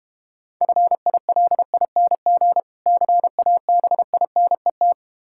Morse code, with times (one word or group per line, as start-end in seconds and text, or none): 0.6-2.6 FILING
2.9-4.9 CABINET